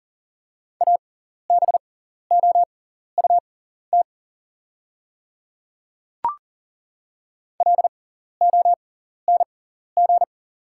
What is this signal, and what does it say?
Morse code, heard as a keyed tone, about 40 wpm, spaced as for 14 wpm, 700 Hz. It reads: ABOUT LONG
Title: 6x Character Spacing - Top 100 Words - 40wpm